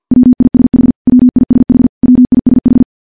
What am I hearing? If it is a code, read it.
Morse code, heard as a keyed tone, about 50 wpm, 250 Hz. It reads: WISH WISH WISH
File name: morse / morse-training-50WPM